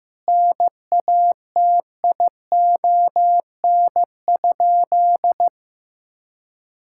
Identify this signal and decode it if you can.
Morse code, heard as a keyed tone, about 14 wpm, 700 Hz. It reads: NATION?
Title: Sentences from Top 1000 Words - Review Part 6 - 15wpm